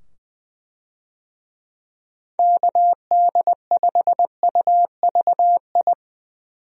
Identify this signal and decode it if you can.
Morse code, heard as a keyed tone, about 20 wpm, 700 Hz. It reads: KD5UVI
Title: USA Call Signs - 6 Characters - 20wpm